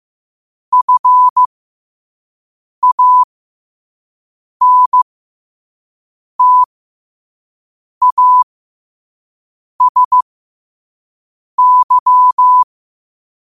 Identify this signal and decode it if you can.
Morse code, heard as a keyed tone, about 15 wpm, spaced as for 5 wpm, 1000 Hz. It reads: FANTASY
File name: qotdfiles / 5wpm